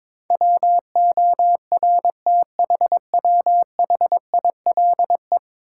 Morse code, heard as a keyed tone, about 22 wpm, 700 Hz. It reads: WORTHWHILE
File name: practice / 22wpm / 10 Letter Words - First 3 Letters Given - 22wpm